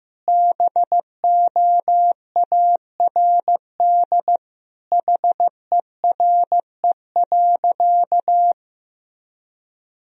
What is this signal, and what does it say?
Morse code, heard as a keyed tone, about 16 wpm, 700 Hz. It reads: BOARD HERE.
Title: Sentences from Top 1000 Words - Review Part 4 - 15wpm